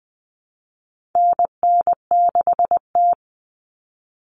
Morse code, heard as a keyed tone, about 20 wpm, 700 Hz. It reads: NN6T
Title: Call Signs - 20wpm